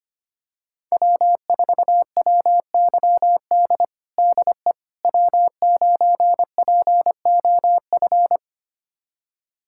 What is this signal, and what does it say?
Morse code, heard as a keyed tone, about 25 wpm, 700 Hz. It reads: W4WYD DE W9POF
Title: USA Callsign DE Callsign - 5 Characters - 25wpm